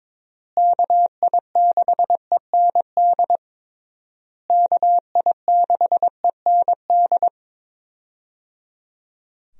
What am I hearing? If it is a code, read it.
Morse code, heard as a keyed tone, about 22 wpm, 700 Hz. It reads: KI6END KI6END
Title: USA Call Signs - 6 Characters - Repeated 3x - 22wpm